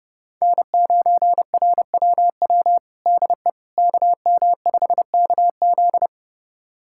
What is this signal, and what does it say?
Morse code, heard as a keyed tone, about 30 wpm, 700 Hz. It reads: N9RWW DE KM5KZ
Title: USA Callsign DE Callsign - 5 Characters - 30wpm